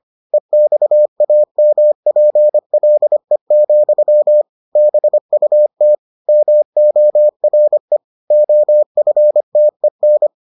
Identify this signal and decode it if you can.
Morse code, heard as a keyed tone, about 25 wpm, 600 Hz. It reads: EXAMPLE, BUT MORE OFTEN